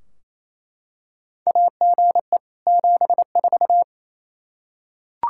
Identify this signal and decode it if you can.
Morse code, heard as a keyed tone, about 28 wpm, 700 Hz. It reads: AGE 74
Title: Practice QSO - Exchange 3 - More Info - 28wpm